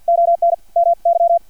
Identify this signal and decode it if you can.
Morse code, heard as a keyed tone, about 50 wpm, 680 Hz. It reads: ON MY
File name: archived / CRSnet